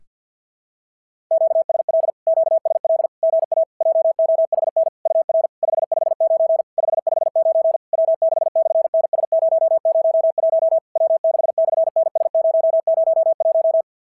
Morse code, heard as a keyed tone, about 46 wpm, 660 Hz. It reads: QSL QSL GA JOHN UR 559 559 W6/NS001 W6/NS001
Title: SOTA QSO Practice 50wpm